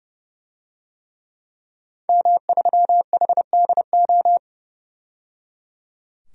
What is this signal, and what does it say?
Morse code, heard as a keyed tone, about 30 wpm, 700 Hz. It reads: M3HDO
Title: DXCC-100 Call Signs - 5 Characters - 30wpm